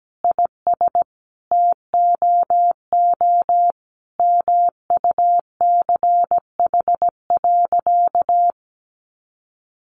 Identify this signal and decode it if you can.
Morse code, heard as a keyed tone, about 17 wpm, 700 Hz. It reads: IS TOO MUCH.
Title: Sentences from Top 400 Words 17wpm